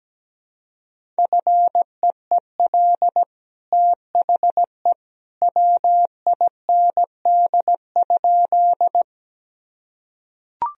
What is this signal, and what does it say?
Morse code, heard as a keyed tone, about 17 wpm, 700 Hz. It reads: FEEL THE WIND?
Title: Sentences from Top 1000 Words - Review Part 2 - 17wpm